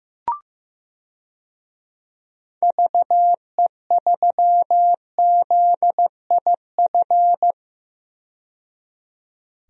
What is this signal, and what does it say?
Morse code, heard as a keyed tone, about 15 wpm, 700 Hz. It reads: VE3ZIF
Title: DXCC-100 Call Signs - 6 Characters - 15wpm